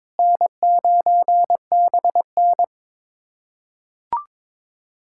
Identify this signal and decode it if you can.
Morse code, heard as a keyed tone, about 22 wpm, 700 Hz. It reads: N9BN
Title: USA Callsign DE Callsign - 4 Characters - 22wpm